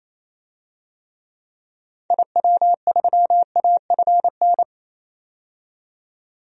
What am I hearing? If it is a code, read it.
Morse code, heard as a keyed tone, about 28 wpm, 700 Hz. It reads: IW3AFN
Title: DXCC-100 Call Signs - 6 Characters - 28wpm